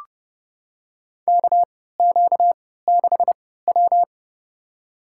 Morse code, heard as a keyed tone, about 32 wpm, spaced as for 17 wpm, 700 Hz. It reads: KQ6W
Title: USA Call Signs - 3x Character Spacing - 30wpm